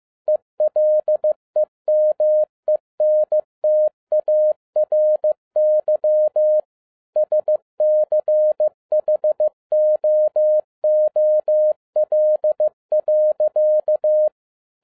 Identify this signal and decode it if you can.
Morse code, heard as a keyed tone, about 15 wpm, 610 Hz. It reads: ELEMENTARY SCHOOL.